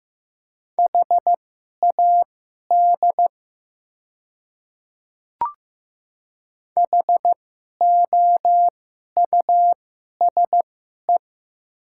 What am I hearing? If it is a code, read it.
Morse code, heard as a keyed tone, about 15 wpm, spaced as for 11 wpm, 700 Hz. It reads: HAD HOUSE
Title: 2x Character Spacing - Top 200 Words - 15wpm